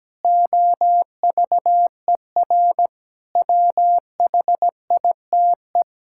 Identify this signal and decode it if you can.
Morse code, heard as a keyed tone, about 17 wpm, 700 Hz. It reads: OVER WHITE
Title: Pangram Challenge v2 - 17wpm